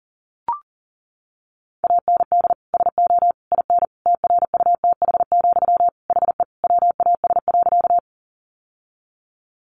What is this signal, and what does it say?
Morse code, heard as a keyed tone, about 40 wpm, 700 Hz. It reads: AND SO IN TRUTH, HE WAS.
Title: Sentences from Top 800 Words 40wpm